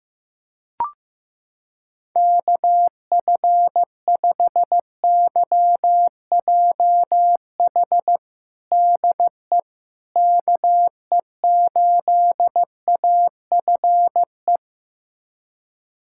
Morse code, heard as a keyed tone, about 15 wpm, 700 Hz. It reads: KF5YJH DE KE8AFE
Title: USA Callsign DE Callsign - 6 Characters - 15wpm